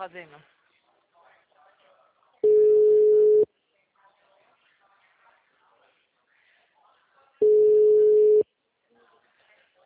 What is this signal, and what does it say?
Ringback tone (Europe, 425 Hz), with speech.